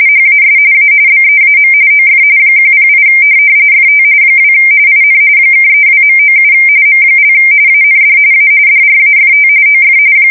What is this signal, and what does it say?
RTTY, about 45.45 baud. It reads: HAS A BEAM OF 108 FT 2 IN (32.97 M) AND A DRAFT OF
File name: fsk170 / USS-Massachusetts